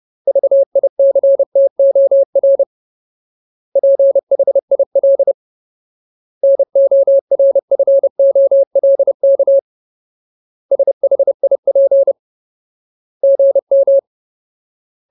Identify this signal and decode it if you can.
Morse code, heard as a keyed tone, about 30 wpm, 550 Hz. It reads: VICTOR PHIL NORFOLK SHIP GM
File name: Ultimate Word List / 30WPM Ultimate